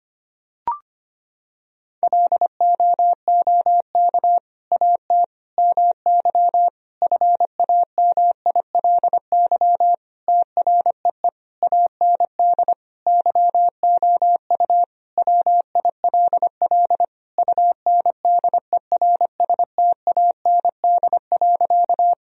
Morse code, heard as a keyed tone, about 25 wpm, 700 Hz. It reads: LOOK AT MY FAMILY TREE AND YOU WILL UNDERSTAND.
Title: Sentences from Top 1000 Words - Review Part 1 - 25wpm